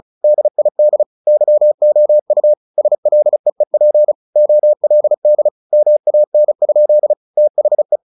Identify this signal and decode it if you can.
Morse code, heard as a keyed tone, about 34 wpm, 600 Hz. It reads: DID YOU SLEEP OLD MAN? THE
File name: the old man and the sea / the old man and the sea 1 35wpm